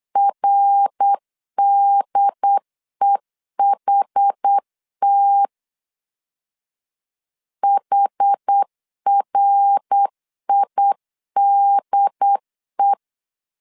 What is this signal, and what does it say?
Morse code, heard as a keyed tone, about 8 wpm, 800 Hz. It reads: RDEHT HRIDE
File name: CW / DET / 1 3 1500C S B 8wpm 4141